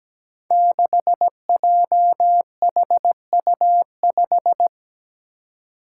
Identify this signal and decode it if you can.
Morse code, heard as a keyed tone, about 17 wpm, 700 Hz. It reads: 6JHU5
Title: Five Challenging Letter-Number Combinations 17wpm